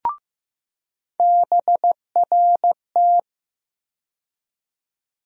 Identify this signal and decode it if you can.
Morse code, heard as a keyed tone, about 15 wpm, 700 Hz. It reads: BRT
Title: USA Call Sign Postfixes - 3 Letters 15wpm